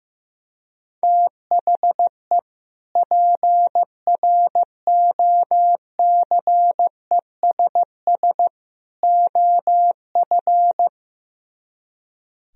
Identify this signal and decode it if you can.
Morse code, heard as a keyed tone, about 15 wpm, 700 Hz. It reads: THE PROCESS OF
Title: Speed-Racing - Sets of 3 Words 15wpm